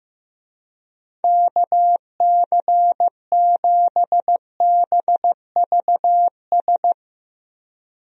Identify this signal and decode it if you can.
Morse code, heard as a keyed tone, about 15 wpm, 700 Hz. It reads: KC7BVS
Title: USA Call Signs - 6 Characters - Repeated 2x - 15wpm